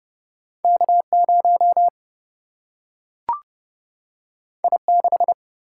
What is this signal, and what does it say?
Morse code, heard as a keyed tone, about 29 wpm, 700 Hz. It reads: K0 I6
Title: Single-Letter Single-Number 30wpm